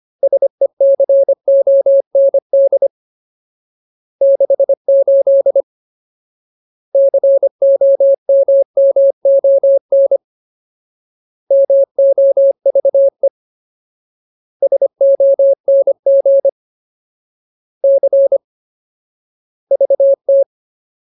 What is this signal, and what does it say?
Morse code, heard as a keyed tone, about 25 wpm, 550 Hz. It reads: SECOND 68 COMMON MOVE SONG C VT
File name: Ultimate Word List / 25WPM Ultimate